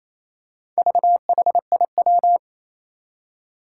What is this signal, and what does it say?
Morse code, heard as a keyed tone, about 28 wpm, 700 Hz. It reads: VHIW
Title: Four Challenging Letter-Number Combinations 28wpm